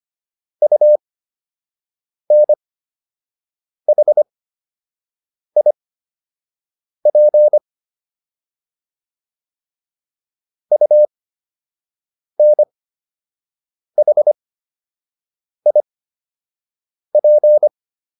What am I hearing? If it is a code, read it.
Morse code, heard as a keyed tone, about 25 wpm, spaced as for 6 wpm, 600 Hz. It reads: UNHIP UNHIP